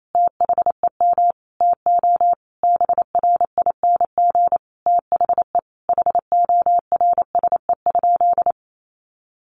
Morse code, heard as a keyed tone, about 28 wpm, 700 Hz. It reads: THEM TO BRING THE HORSE?
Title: Sentences from Top 500 Words 28wpm